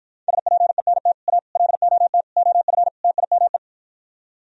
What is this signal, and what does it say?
Morse code, heard as a keyed tone, about 53 wpm, 700 Hz. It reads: SPENT A LOT OF TIME